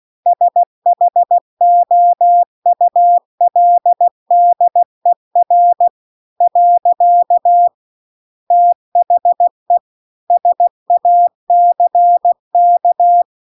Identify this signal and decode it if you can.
Morse code, heard as a keyed tone, about 16 wpm, 700 Hz. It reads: SHOULDER . THE SACK
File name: CW-The-Big-Twoheaded-River-2-16WPM